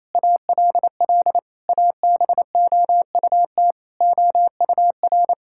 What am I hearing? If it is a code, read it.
Morse code, heard as a keyed tone, about 28 wpm, 700 Hz. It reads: ALL ABOUT OUR
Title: Sentences from Top 1000 Words - Review Part 5 - 28wpm